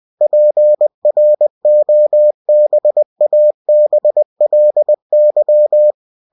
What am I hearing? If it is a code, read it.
Morse code, heard as a keyed tone, about 20 wpm, 600 Hz. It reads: PROBABLY
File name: cw-books / winnie-the-pooh / winnie-20wpm-page-0074